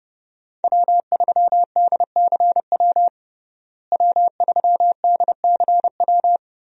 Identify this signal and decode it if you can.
Morse code, heard as a keyed tone, about 30 wpm, 700 Hz. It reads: W3DCW W3DCW
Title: USA Call Signs - 5 Characters - Repeated 3x - 30wpm